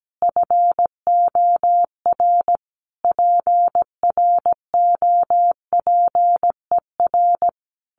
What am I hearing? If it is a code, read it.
Morse code, heard as a keyed tone, about 17 wpm, 700 Hz. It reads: FOR PROPER
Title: Sentences from Top 1500 Words 17wpm